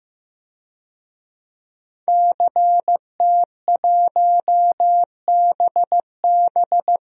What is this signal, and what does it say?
Morse code, heard as a keyed tone, about 16 wpm, 700 Hz. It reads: CT1BB
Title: DXCC-100 Call Signs - 5 Characters - Repeated 2x - 15wpm